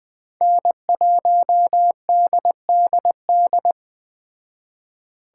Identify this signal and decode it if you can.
Morse code, heard as a keyed tone, about 21 wpm, 700 Hz. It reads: N1DDD